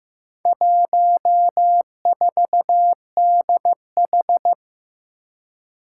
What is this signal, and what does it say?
Morse code, heard as a keyed tone, about 15 wpm, 700 Hz. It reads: 14DH
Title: Four Challenging Letter-Number Combinations 15wpm